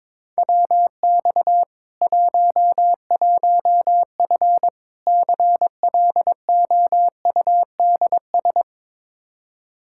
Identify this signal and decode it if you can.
Morse code, heard as a keyed tone, about 22 wpm, 700 Hz. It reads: WX 11F CLOUDS
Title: Practice QSO - Exchange 2 - Info - 22wpm